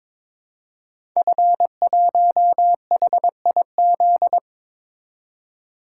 Morse code, heard as a keyed tone, about 22 wpm, 700 Hz. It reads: F1HIZ
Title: Code Groups - 5 Letter:Numbers - 22wpm